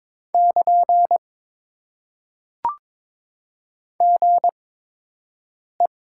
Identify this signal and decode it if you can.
Morse code, heard as a keyed tone, about 22 wpm, spaced as for 6 wpm, 700 Hz. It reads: <KN> GE
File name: Top 90 QSO Elements - 8x Character Spacing - 22wpm